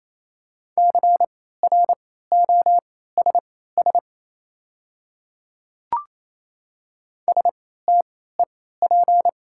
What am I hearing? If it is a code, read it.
Morse code, heard as a keyed tone, about 28 wpm, spaced as for 16 wpm, 690 Hz. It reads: CROSS STEP
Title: 3x Character Spacing - Top 500 Words - 28wpm